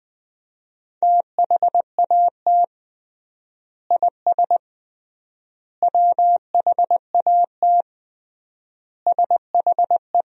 Morse code, heard as a keyed tone, about 20 wpm, 700 Hz. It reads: THAT IS WHAT SHE
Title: 3x Word-Spacing - Sentences from Top 100 Words - 20wpm